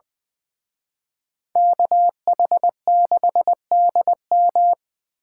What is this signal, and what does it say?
Morse code, heard as a keyed tone, about 24 wpm, 700 Hz. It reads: KH6DM
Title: Speed Racing - USA 2x2 Call Signs 20wpm